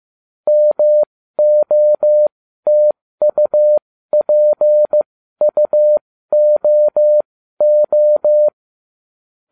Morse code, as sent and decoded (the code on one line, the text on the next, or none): -- --- - ..- .--. ..- --- ---
MOTUPUOO